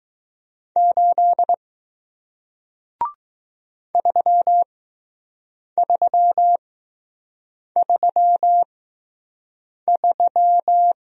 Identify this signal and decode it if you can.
Morse code, heard as a keyed tone, about 20 wpm, spaced as for 6 wpm, 700 Hz. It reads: 8 3333